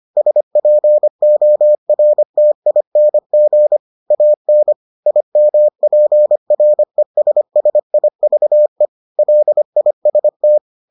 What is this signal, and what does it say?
Morse code, heard as a keyed tone, about 25 wpm, 600 Hz. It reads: SPORTING AN IMPRESSIVE LIST